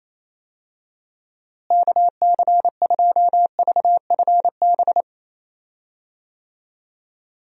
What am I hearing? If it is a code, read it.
Morse code, heard as a keyed tone, about 28 wpm, 700 Hz. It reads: KC2VFB
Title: USA Call Signs - 6 Characters - Repeated 2x - 28wpm